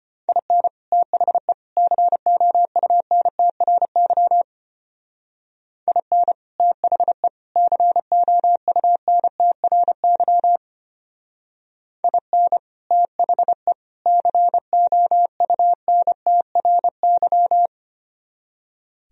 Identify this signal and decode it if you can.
Morse code, heard as a keyed tone, about 31 wpm, 700 Hz. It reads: IN THE COUNTRY IN THE COUNTRY IN THE COUNTRY
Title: Speed-Racing - Sets of 3 Words 25wpm